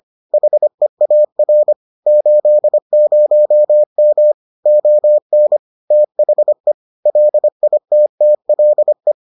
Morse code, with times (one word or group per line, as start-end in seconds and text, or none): 0.3-1.7 HEAR
2.1-4.3 80M
4.7-5.6 ON
5.9-6.7 THE
7.1-9.1 LITTLE